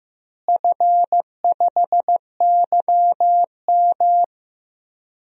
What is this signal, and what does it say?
Morse code, heard as a keyed tone, about 16 wpm, 700 Hz. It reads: F5YM
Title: DXCC-100 Call Signs - 4 Characters - Repeated 2x - 15wpm